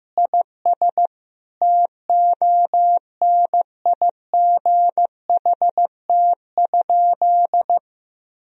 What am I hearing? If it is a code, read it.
Morse code, heard as a keyed tone, about 15 wpm, 700 Hz. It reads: IS TONIGHT?
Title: Sentences from Top 1000 Words - Review Part 3 - 15wpm